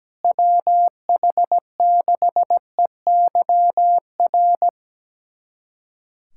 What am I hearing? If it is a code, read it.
Morse code, heard as a keyed tone, about 17 wpm, 700 Hz. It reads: WH6EYR
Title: USA Callsign DE Callsign 17wpm